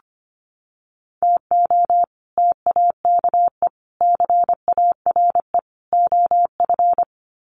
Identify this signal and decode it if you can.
Morse code, heard as a keyed tone, about 25 wpm, 700 Hz. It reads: TO TAKE CARE OF
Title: Sets of 4 Words 25wpm